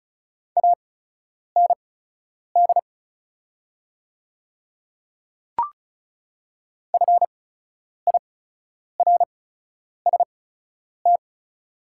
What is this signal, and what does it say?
Morse code, heard as a keyed tone, about 36 wpm, spaced as for 10 wpm, 700 Hz. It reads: AND FIRST
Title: 8x Character Spacing - Top 100 Words - 35wpm